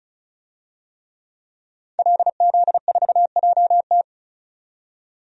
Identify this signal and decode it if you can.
Morse code, heard as a keyed tone, about 34 wpm, 700 Hz. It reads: LZ4JT
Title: DXCC-100 Call Signs - 5 Characters - 35wpm